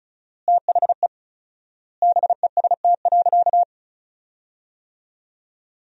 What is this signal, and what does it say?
Morse code, heard as a keyed tone, about 36 wpm, 700 Hz. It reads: THE BEST.
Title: 4x Word-Spacing - Sentences from Top 500 Words - 35wpm